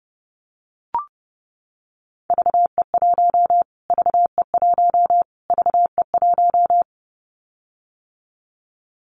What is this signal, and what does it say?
Morse code, heard as a keyed tone, about 30 wpm, 700 Hz. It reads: VE1 VE1 VE1
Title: Callsign Prefixes for Top 25 DXCC - Repeated 3x - 30wpm